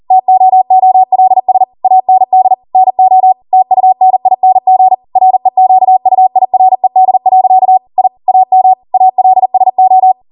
Morse code, heard as a keyed tone, about 40 wpm, 750 Hz. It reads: TOOLS AND NO TUNING REQUIRED. I AM ALSO
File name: Morse / 40WPM